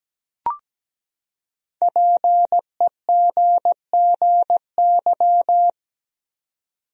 Morse code, heard as a keyed tone, about 17 wpm, 700 Hz. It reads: PEGGY
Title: USA Common Names - 17wpm